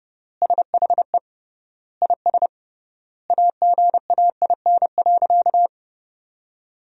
Morse code, heard as a keyed tone, about 30 wpm, 700 Hz. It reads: SHE IS AGAIN.
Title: 3x Word-Spacing - Sentences from Top 200 Words - 30wpm